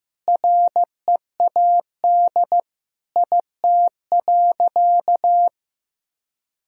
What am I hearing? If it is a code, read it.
Morse code, heard as a keyed tone, about 16 wpm, 700 Hz. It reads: READ IT.
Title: Sentences from Top 1000 Words - Review Part 4 - 15wpm